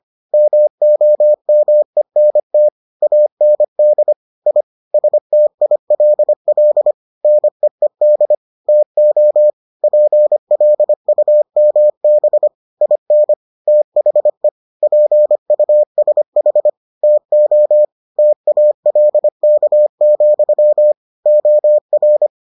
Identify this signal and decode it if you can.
Morse code, heard as a keyed tone, about 25 wpm, 600 Hz. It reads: MOMENT AND I STILL NEED TO PLUMB IN THE PUSH TO TALK, OR